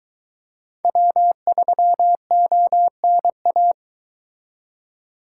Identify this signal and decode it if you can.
Morse code, heard as a keyed tone, about 23 wpm, 700 Hz. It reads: W3ONA